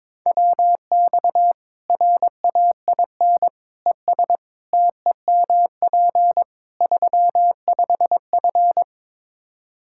Morse code, heard as a keyed tone, about 22 wpm, 700 Hz. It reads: WX RAIN ES TEMP 35F
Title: Practice QSO - Exchange 2 - Info - 22wpm